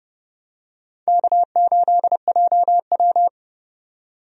Morse code, heard as a keyed tone, about 30 wpm, 700 Hz. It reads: K8JW
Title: USA Call Signs - 4 Characters - 30wpm